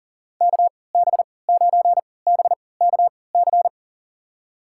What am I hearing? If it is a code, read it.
Morse code, heard as a keyed tone, about 39 wpm, spaced as for 22 wpm, 700 Hz. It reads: KB9BKC
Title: USA Call Signs - 6 Characters - 3x Character Spacing - 40wpm